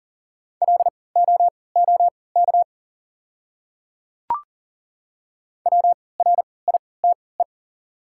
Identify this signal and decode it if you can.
Morse code, heard as a keyed tone, about 40 wpm, spaced as for 23 wpm, 700 Hz. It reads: LOOK WRITE
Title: 3x Character Spacing - Top 100 Words - 40wpm